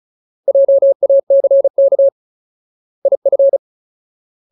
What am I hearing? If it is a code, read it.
Morse code, heard as a keyed tone, about 34 wpm, 550 Hz. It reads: JACK IF
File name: Ultimate Word List / 35WPM Ultimate